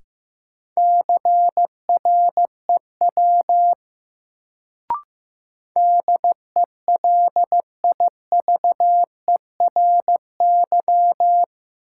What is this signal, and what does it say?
Morse code, heard as a keyed tone, about 15 wpm, 700 Hz. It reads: CREW DELIVERY
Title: Sentences from Top 1500 Words - New Word Drill Alt - 15wpm